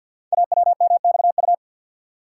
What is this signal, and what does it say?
Morse code, heard as a keyed tone, about 48 wpm, 690 Hz. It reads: AWMXU
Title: Code Groups - 5 Letters - 50wpm